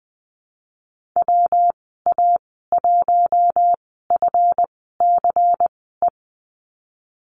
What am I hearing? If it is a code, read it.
Morse code, heard as a keyed tone, about 20 wpm, spaced as for 15 wpm, 700 Hz. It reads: WA1FCE
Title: 173 Lesson 28g - Number 9 - Call Signs 20wpm